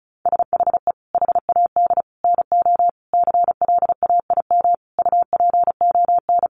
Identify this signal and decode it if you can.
Morse code, heard as a keyed tone, about 34 wpm, 700 Hz. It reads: SHE HAD NO CLAIM UPON
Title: Sentences from Top 1500 Words 35wpm